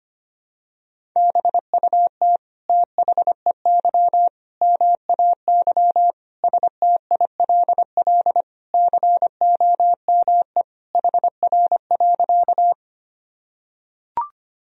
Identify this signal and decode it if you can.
Morse code, heard as a keyed tone, about 25 wpm, 700 Hz. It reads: BUT THEY MAY STILL COME HR.
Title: Sentences from Top 300 Words and Abbreviations 25wpm